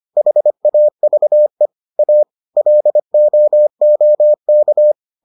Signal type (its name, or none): Morse code